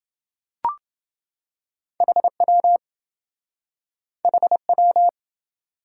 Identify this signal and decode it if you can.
Morse code, heard as a keyed tone, about 27 wpm, 700 Hz. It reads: HW HW